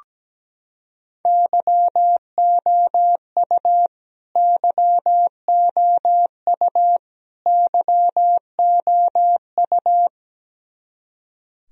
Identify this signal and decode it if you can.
Morse code, heard as a keyed tone, about 17 wpm, 700 Hz. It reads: YOU YOU YOU
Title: Top 100 Words Repeated 17wpm